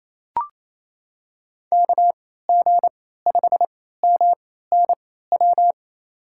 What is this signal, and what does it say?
Morse code, heard as a keyed tone, about 28 wpm, spaced as for 16 wpm, 700 Hz. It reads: KG5MNW
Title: USA Call Signs - 6 Characters - 3x Character Spacing - 28wpm